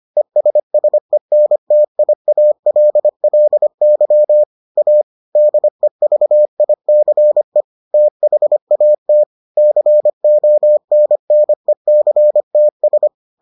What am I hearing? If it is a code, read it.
Morse code, heard as a keyed tone, about 25 wpm, 600 Hz. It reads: ESSENTIALLY A DEVICE THAT CONNECTS